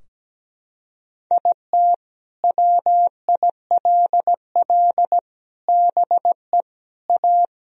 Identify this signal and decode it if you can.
Morse code, heard as a keyed tone, about 17 wpm, 700 Hz. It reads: IT WILL BE A
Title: Sentences from Top 1000 Words - Review Part 3 - 17wpm